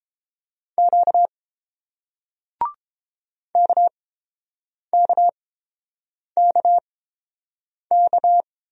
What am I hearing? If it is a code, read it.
Morse code, heard as a keyed tone, about 30 wpm, spaced as for 7 wpm, 700 Hz. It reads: Q KKKK